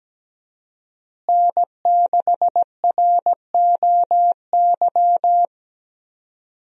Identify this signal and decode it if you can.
Morse code, heard as a keyed tone, about 17 wpm, 700 Hz. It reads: N6ROY
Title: USA Call Signs - 5 Characters - Repeated 3x - 17wpm